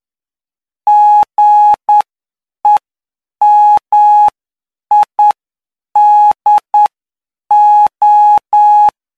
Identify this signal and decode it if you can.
Morse code, heard as a keyed tone, about 9 wpm, spaced as for 7 wpm, 840 Hz. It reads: GEMIDO